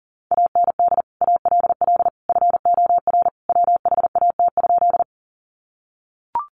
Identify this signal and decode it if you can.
Morse code, heard as a keyed tone, about 40 wpm, 700 Hz. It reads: AND ALL FOR WHAT?
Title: Sentences from Top 100 Words 40wpm